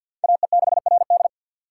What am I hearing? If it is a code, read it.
Morse code, heard as a keyed tone, about 53 wpm, 700 Hz. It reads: AE6RD
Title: USA Call Signs - 5 Characters - 6x Character Spacing - 50wpm